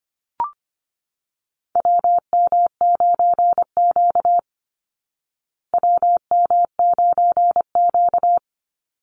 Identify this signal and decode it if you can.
Morse code, heard as a keyed tone, about 25 wpm, 700 Hz. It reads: WM9Q WM9Q
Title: Call Signs Repeated 3x - 25wpm